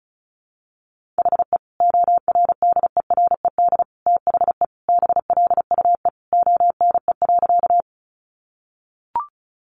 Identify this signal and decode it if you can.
Morse code, heard as a keyed tone, about 34 wpm, 700 Hz. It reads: HE ORDERED THE BLUE ONE.
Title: Additional Easy Phrases 35wpm